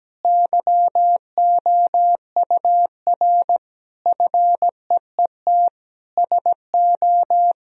Morse code, heard as a keyed tone, about 17 wpm, 700 Hz. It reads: YOUR FEET SO
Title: Sentences from Top 1000 Words - Review Part 2 - 17wpm